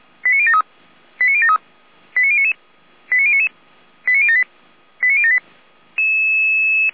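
Five-tone selective call (ZVEI), heard as quick runs of five tones, tones of about 80 ms. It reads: 89073 89073 89099 89078 89078
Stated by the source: ZVEI five-tone call received on an SDR.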